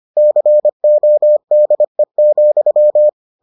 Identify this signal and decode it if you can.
Morse code, heard as a keyed tone, about 25 wpm, 600 Hz. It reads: CODE,